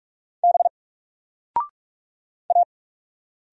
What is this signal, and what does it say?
Morse code, heard as a keyed tone, about 46 wpm, 700 Hz. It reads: B A
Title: Single Letter-Number 45wpm